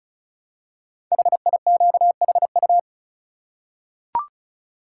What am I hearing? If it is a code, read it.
Morse code, heard as a keyed tone, about 36 wpm, 700 Hz. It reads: HIQHU
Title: Code Groups - 5 Letters - 35wpm